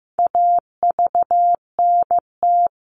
Morse code, heard as a keyed tone, about 15 wpm, 700 Hz. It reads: AVNT
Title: All Cut Numbers - 4-Digit 15wpm